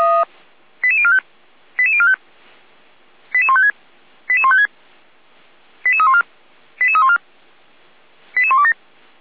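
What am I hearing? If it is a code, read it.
Five-tone selective call (ZVEI), heard as quick runs of five tones, tones of about 70 ms. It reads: 88045 88045 88156 88156 88314 88314 88127
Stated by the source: ZVEI five-tone call 88247 received on an SDR, from the Feuerwehr Orlamünde fire brigade server.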